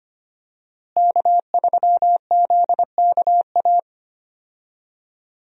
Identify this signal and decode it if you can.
Morse code, heard as a keyed tone, about 25 wpm, 700 Hz. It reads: K3ZKA
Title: USA Call Signs - 5 Characters - Repeated 2x - 25wpm